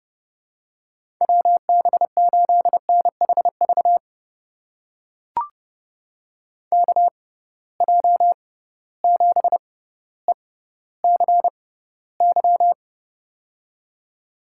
Morse code, heard as a keyed tone, about 31 wpm, spaced as for 16 wpm, 700 Hz. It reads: WB8NHV KJ7ECY
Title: USA Call Signs - 6 Characters - 6x Character Spacing - 30wpm